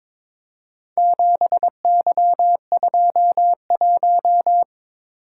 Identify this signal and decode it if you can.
Morse code, heard as a keyed tone, about 23 wpm, 700 Hz. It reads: 7Y21
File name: practice / 22wpm / Four Letter-Number Combinations - 22wpm